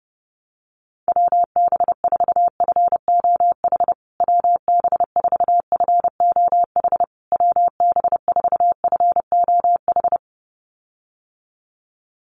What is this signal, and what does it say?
Morse code, heard as a keyed tone, about 30 wpm, 700 Hz. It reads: WB4FOH WB4FOH WB4FOH
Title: USA Call Signs Repeated 3x 30wpm